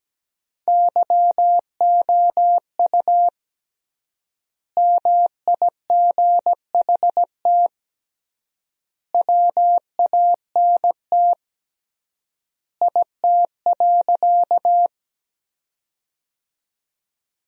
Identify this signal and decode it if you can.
Morse code, heard as a keyed tone, about 17 wpm, 700 Hz. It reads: YOU MIGHT WANT IT.
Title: 3x Word-Spacing - Sentences from Top 300 Words - 17wpm